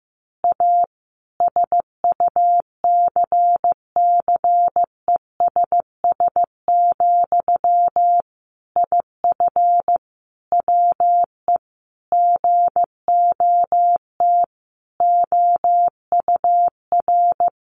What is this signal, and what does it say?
Morse code, heard as a keyed tone, about 15 wpm, 700 Hz. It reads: A SUCCESS, IF WE GOT OUR
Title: A Sound Of Thunder 15wpm